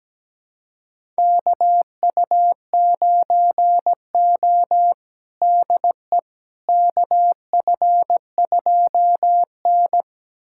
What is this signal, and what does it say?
Morse code, heard as a keyed tone, about 17 wpm, 700 Hz. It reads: KU9O DE KF2N
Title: USA Callsign DE Callsign - 4 Characters - 17wpm